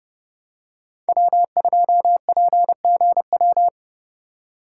Morse code, heard as a keyed tone, about 29 wpm, 700 Hz. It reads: W2PGW